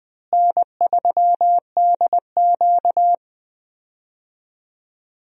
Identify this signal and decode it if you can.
Morse code, heard as a keyed tone, about 20 wpm, 700 Hz. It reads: N3DQ
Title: USA Call Signs - 4 Characters - Repeated 2x - 20wpm